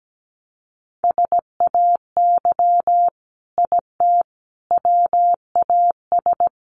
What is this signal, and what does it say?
Morse code, heard as a keyed tone, about 17 wpm, 700 Hz. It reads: SAY IT WAS
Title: Sentences from Top 800 Words 17wpm